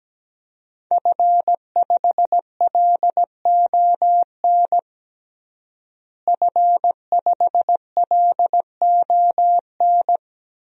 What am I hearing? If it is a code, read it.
Morse code, heard as a keyed tone, about 17 wpm, 700 Hz. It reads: F5LON F5LON
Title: DXCC-100 Call Signs - 5 Characters - Repeated 2x - 17wpm